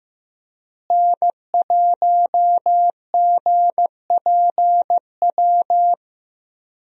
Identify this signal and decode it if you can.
Morse code, heard as a keyed tone, about 16 wpm, 700 Hz. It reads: N1GPW
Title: USA Call Signs - 5 Characters - Repeated 2x - 15wpm